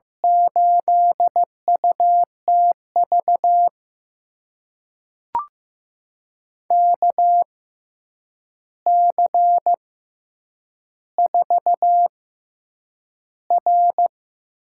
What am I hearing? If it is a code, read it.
Morse code, heard as a keyed tone, about 15 wpm, spaced as for 8 wpm, 700 Hz. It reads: 8UTV KC4R
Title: USA Call Signs - 6 Characters - 6x Character Spacing - 15wpm